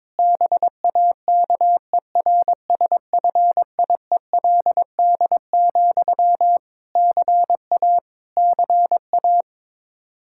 Morse code, heard as a keyed tone, about 22 wpm, 700 Hz. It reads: BAKERSFIELD, CA CA
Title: Practice QSO - Exchange 1 - Beginning - 22wpm